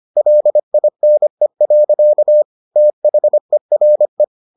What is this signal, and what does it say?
Morse code, heard as a keyed tone, about 25 wpm, 600 Hz. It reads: LINE. THERE